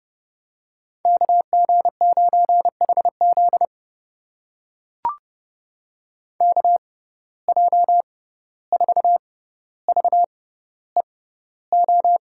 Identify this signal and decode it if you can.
Morse code, heard as a keyed tone, about 30 wpm, spaced as for 15 wpm, 700 Hz. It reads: KG9HZ KJ4VEO
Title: USA Call Signs - 6x Character Spacing - 30wpm